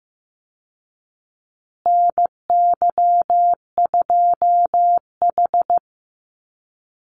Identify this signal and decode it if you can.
Morse code, heard as a keyed tone, about 15 wpm, 700 Hz. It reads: NY2H
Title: Call Signs Repeated 2x - 15wpm